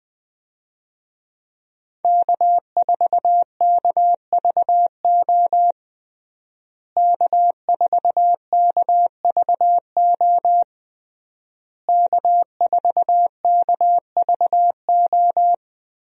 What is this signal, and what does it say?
Morse code, heard as a keyed tone, about 20 wpm, 700 Hz. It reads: K4KVO K4KVO K4KVO
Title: USA Call Signs - 5 Characters - Repeated 3x - 20wpm